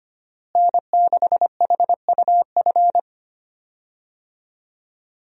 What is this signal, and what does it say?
Morse code, heard as a keyed tone, about 25 wpm, 700 Hz. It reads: N6HUF
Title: USA Call Signs - 5 Characters - Repeated 2x - 25wpm